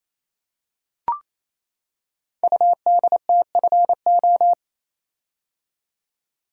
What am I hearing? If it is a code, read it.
Morse code, heard as a keyed tone, about 29 wpm, 700 Hz. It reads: UDTFO
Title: Code Groups - 5 Letters - 28wpm